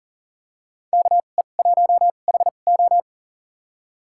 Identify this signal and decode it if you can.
Morse code, heard as a keyed tone, about 40 wpm, spaced as for 29 wpm, 700 Hz. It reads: KE1HO
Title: USA Call Signs - 5 Characters - 2x Character Spacing - 40wpm